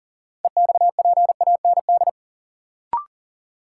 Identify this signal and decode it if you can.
Morse code, heard as a keyed tone, about 37 wpm, 700 Hz. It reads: EXPAND
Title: Sentences from Top 1400 Words - New Word Drill Alt - 40wpm